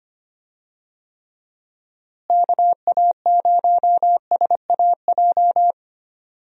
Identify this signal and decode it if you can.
Morse code, heard as a keyed tone, about 25 wpm, 700 Hz. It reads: KA0SAJ
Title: USA Call Signs - 6 Characters - Repeated 3x - 25wpm